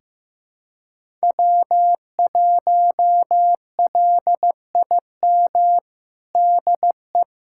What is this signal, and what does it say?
Morse code, heard as a keyed tone, about 15 wpm, 700 Hz. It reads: W1LIM DE
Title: USA Callsign DE Callsign - 5 Characters - 15wpm